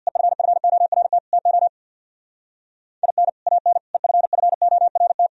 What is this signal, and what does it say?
Morse code, heard as a keyed tone, about 64 wpm, 700 Hz. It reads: EFFORT TO IN AN EFFORT